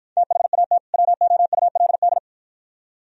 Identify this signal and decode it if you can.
Morse code, heard as a keyed tone, about 53 wpm, 700 Hz. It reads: THAT WOULD